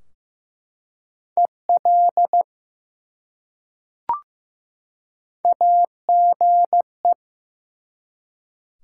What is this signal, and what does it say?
Morse code, heard as a keyed tone, about 15 wpm, 700 Hz. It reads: EL AGE
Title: Top 180 QSO Elements 15wpm